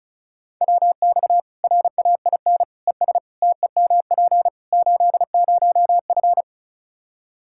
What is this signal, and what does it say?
Morse code, heard as a keyed tone, about 34 wpm, 700 Hz. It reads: WX RAIN ES TEMP 80F
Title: K1USN SST QSO Practice 35wpm